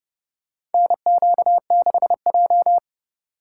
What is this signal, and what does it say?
Morse code, heard as a keyed tone, about 31 wpm, 700 Hz. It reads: NQ6J
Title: USA Call Signs - 4 Characters - 3x Character Spacing - 30wpm